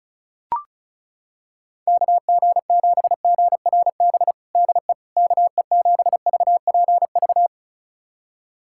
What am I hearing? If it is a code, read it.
Morse code, heard as a keyed tone, about 36 wpm, 700 Hz. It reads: KG7GRB DE KE7VPV
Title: USA Callsign DE Callsign - 6 Characters - 35wpm